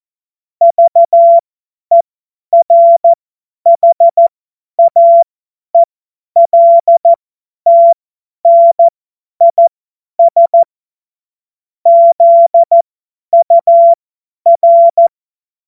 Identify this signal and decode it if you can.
Morse code, heard as a keyed tone, about 14 wpm, spaced as for 10 wpm, 680 Hz. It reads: VERHAELTNIS ZUR
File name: Morsekurs / Witze / Witz 04. Mai 2021, 50 F70